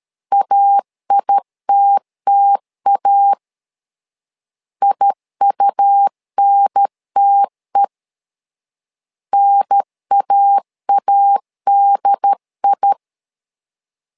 Morse code, heard as a keyed tone, about 13 wpm, 800 Hz. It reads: AITTA IUNTE NAADI
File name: LEZIONI / 2 UHDN / 2 UHDN 12wpm 1456